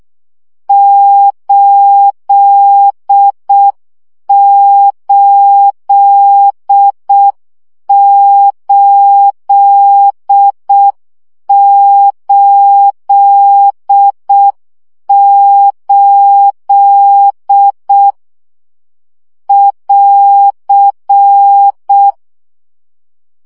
Morse code, heard as a keyed tone, about 6 wpm, 800 Hz. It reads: 88888 <AR>
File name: CW gyakorlas 8 5 baud